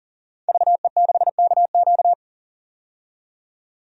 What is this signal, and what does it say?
Morse code, heard as a keyed tone, about 37 wpm, 700 Hz. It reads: VE6KQ
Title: DXCC-100 Call Signs - 5 Characters - 40wpm